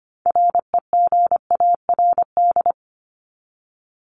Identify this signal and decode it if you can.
Morse code, heard as a keyed tone, about 25 wpm, 700 Hz. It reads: REGARD
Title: ING Suffix - Top 100 Words with Base Words 25wpm